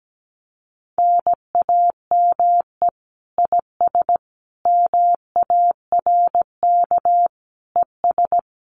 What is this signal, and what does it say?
Morse code, heard as a keyed tone, about 17 wpm, 700 Hz. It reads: NAME IS MARK ES
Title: Common QSO Phrases Encore 17wpm